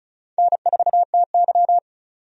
Morse code, heard as a keyed tone, about 36 wpm, 700 Hz. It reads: N4TY
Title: USA Call Signs - 4 Characters - 35wpm